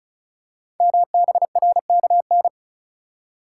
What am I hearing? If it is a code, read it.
Morse code, heard as a keyed tone, about 36 wpm, 700 Hz. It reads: MBRKN
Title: Code Groups - 5 Letters - 35wpm